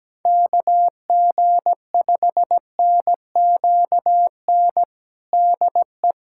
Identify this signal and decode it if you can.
Morse code, heard as a keyed tone, about 17 wpm, 700 Hz. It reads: KG5NQN DE
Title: USA Callsign DE Callsign 17wpm